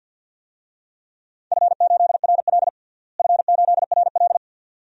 Morse code, heard as a keyed tone, about 47 wpm, 700 Hz. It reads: F8RL F8RL
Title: DXCC-100 Call Signs - 4 Characters - Repeated 2x - 50wpm